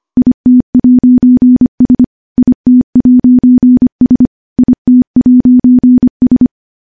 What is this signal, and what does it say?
Morse code, heard as a keyed tone, about 25 wpm, 250 Hz. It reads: IT'S IT'S IT'S
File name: morse / morse-training-25WPM